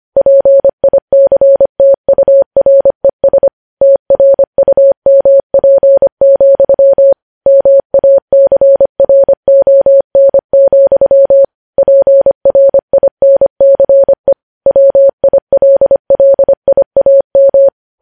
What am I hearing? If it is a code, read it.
Morse code, heard as a keyed tone, about 25 wpm, 550 Hz. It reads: PICTURES TRUMP, MACRON, PRINCE WILLIAM
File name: podcast / News headlines-1x-25x25